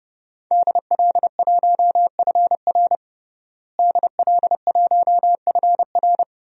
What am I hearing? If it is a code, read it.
Morse code, heard as a keyed tone, about 30 wpm, 700 Hz. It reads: DL1FR DL1FR
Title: DXCC-100 Call Signs - 5 Characters - Repeated 2x - 30wpm